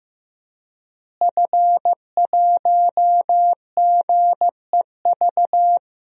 Morse code, heard as a keyed tone, about 15 wpm, 700 Hz. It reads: F1GEV
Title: DXCC-100 Call Signs - 5 Characters - Repeated 3x - 15wpm